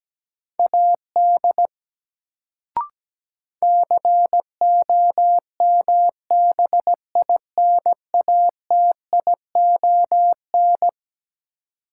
Morse code, heard as a keyed tone, about 17 wpm, 700 Hz. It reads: AD COMBINATION
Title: Sentences from Top 1300 Words - New Word Drill Alt - 17wpm